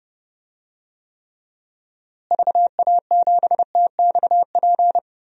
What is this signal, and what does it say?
Morse code, heard as a keyed tone, about 30 wpm, 700 Hz. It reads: VA7TXP
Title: DXCC-100 Call Signs - 6 Characters - Repeated 2x - 30wpm